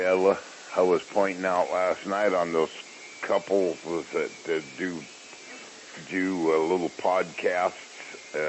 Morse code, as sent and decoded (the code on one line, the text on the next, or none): none